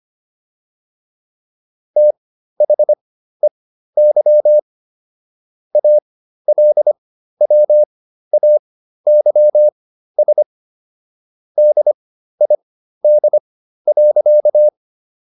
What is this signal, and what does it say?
Morse code, heard as a keyed tone, about 25 wpm, spaced as for 13 wpm, 600 Hz. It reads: THEY ALWAYS DID.